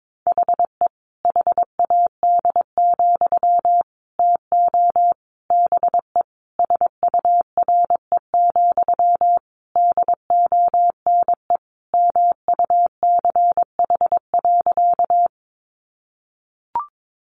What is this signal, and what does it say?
Morse code, heard as a keyed tone, about 22 wpm, 700 Hz. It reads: HE HAD, TO BE SURE, DONE MUCH.
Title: Sentences from Top 500 Words 22wpm